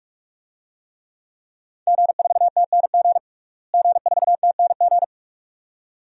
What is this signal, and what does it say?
Morse code, heard as a keyed tone, about 44 wpm, 700 Hz. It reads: G4TNG G4TNG
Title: DXCC-100 Call Signs - 5 Characters - Repeated 2x - 45wpm